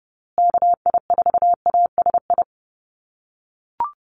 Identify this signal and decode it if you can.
Morse code, heard as a keyed tone, about 30 wpm, 700 Hz. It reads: KI4ASI
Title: USA Call Signs Repeated 3x 30wpm